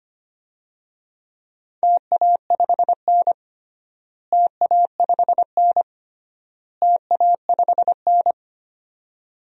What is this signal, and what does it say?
Morse code, heard as a keyed tone, about 25 wpm, 700 Hz. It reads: TA5N TA5N TA5N
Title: DXCC-100 Call Signs - 4 Characters - Repeated 3x - 25wpm